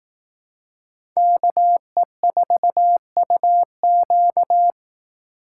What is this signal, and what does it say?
Morse code, heard as a keyed tone, about 18 wpm, 700 Hz. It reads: KE4UQ
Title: Speed Racing - USA Call Signs 15wpm